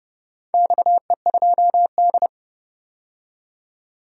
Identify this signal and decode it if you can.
Morse code, heard as a keyed tone, about 29 wpm, 700 Hz. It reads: XE2D
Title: DXCC-100 Call Signs - 4 Characters - 30wpm